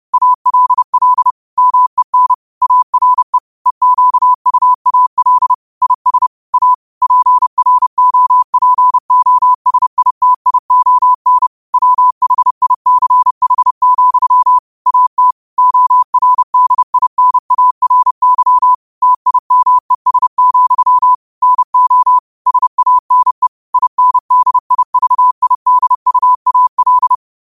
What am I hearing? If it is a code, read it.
Morse code, heard as a keyed tone, about 30 wpm, 1000 Hz. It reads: ALL MEN ARE EQUAL IS A PROPOSITION WHICH, AT ORDINARY TIMES, NO SANE INDIVIDUAL